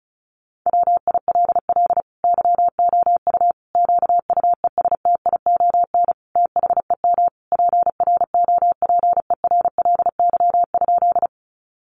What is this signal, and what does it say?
Morse code, heard as a keyed tone, about 34 wpm, 700 Hz. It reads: WILL YOU QUESTION THEM PROPERLY?
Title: Sentences from Top 1100 Words 35wpm